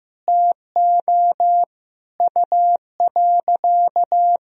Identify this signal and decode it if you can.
Morse code, heard as a keyed tone, about 15 wpm, 700 Hz. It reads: TO U.
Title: Sentences from Top 500 Words and Abbreviations 15wpm